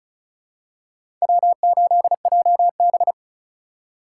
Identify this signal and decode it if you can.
Morse code, heard as a keyed tone, about 34 wpm, 700 Hz. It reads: W8JB
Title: USA Call Signs - 2x Character Spacing - 35wpm